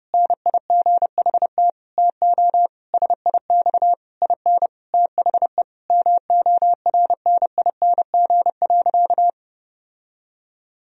Morse code, heard as a keyed tone, about 32 wpm, 700 Hz. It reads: NIGHT TO SIX IN THE MORNING.
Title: Sentences from Top 1000 Words - Review Part 2 - 30wpm